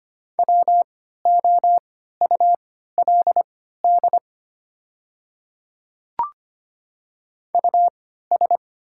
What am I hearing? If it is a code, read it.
Morse code, heard as a keyed tone, about 25 wpm, spaced as for 14 wpm, 700 Hz. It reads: WOULD US
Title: 3x Character Spacing - Top 300 Words - 25wpm